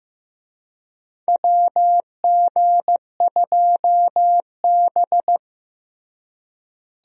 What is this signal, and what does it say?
Morse code, heard as a keyed tone, about 15 wpm, 700 Hz. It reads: WG2B
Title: USA Call Signs - 4 Characters - Repeated 2x - 15wpm